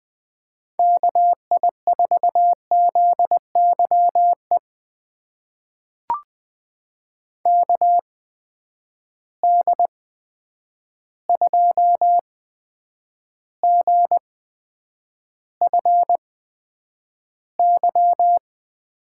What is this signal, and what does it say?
Morse code, heard as a keyed tone, about 20 wpm, spaced as for 9 wpm, 700 Hz. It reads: KI4ZYE KD2GFY
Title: USA Call Signs - 8x Character Spacing - 20wpm